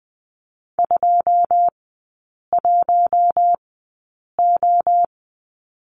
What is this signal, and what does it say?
Morse code, heard as a keyed tone, about 20 wpm, spaced as for 8 wpm, 700 Hz. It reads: 21O
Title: 093 Lesson 16e - Letter c - Three Characters 20wpm